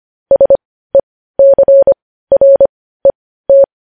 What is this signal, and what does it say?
Morse code, heard as a keyed tone, about 25 wpm, spaced as for 15 wpm, 550 Hz. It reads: SECRET
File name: podcast / News headlines-1x-25x15